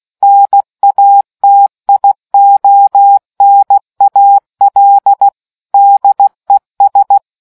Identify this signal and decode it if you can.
Morse code, heard as a keyed tone, about 16 wpm, 800 Hz. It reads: NATIONAL DES